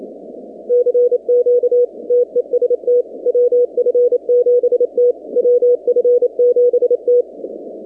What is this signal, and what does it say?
Morse code, heard as a keyed tone, about 27 wpm, 490 Hz. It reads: CQ TEST WF7T WF7T
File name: 2019-cw-wpx-kt5j-k5pi